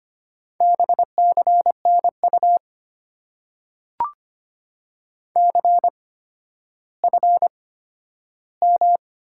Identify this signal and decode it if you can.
Morse code, heard as a keyed tone, about 25 wpm, spaced as for 12 wpm, 700 Hz. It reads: BCNU CFM
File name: Top 90 QSO Elements - 8x Character Spacing - 25wpm